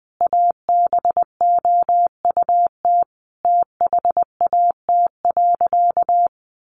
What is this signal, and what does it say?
Morse code, heard as a keyed tone, about 20 wpm, 700 Hz. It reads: ABOUT THAT.
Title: Sentences from Top 1000 Words 20wpm